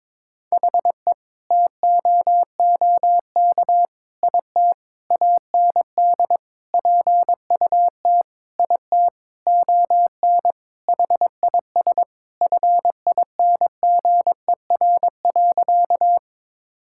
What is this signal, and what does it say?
Morse code, heard as a keyed tone, about 22 wpm, 700 Hz. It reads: HE TOOK IT AND PUT IT ON HIS FINGER.
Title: Sentences from Top 1600 Words 22wpm